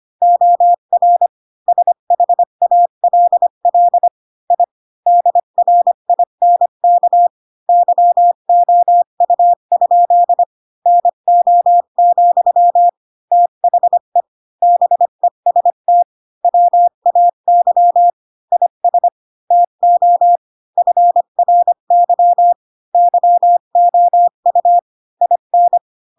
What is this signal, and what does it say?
Morse code, heard as a keyed tone, about 25 wpm, 700 Hz. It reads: OR SHALL I DRINK YOU? NO, THE BEST WAY IS TO FRY YOU IN
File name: Pinocchio / Pinocchio ch5 20 25 ik1hge